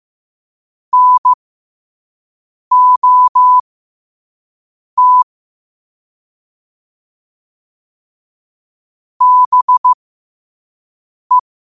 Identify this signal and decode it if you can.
Morse code, heard as a keyed tone, about 15 wpm, spaced as for 5 wpm, 1000 Hz. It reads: NOT BE
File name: qotdfiles / 5wpm